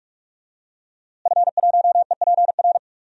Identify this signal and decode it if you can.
Morse code, heard as a keyed tone, about 42 wpm, 700 Hz. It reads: F1EPR
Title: DXCC-100 Call Signs - 5 Characters - Repeated 3x - 45wpm